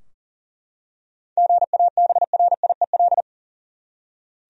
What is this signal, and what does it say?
Morse code, heard as a keyed tone, about 37 wpm, 700 Hz. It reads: GABRIEL
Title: USA Common Names - 40wpm